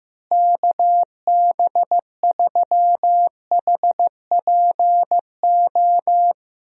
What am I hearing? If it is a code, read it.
Morse code, heard as a keyed tone, about 15 wpm, 700 Hz. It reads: KB3HPO
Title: USA Call Signs - 6 Characters - Repeated 3x - 15wpm